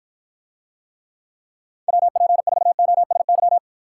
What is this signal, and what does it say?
Morse code, heard as a keyed tone, about 54 wpm, 700 Hz. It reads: WP4OSY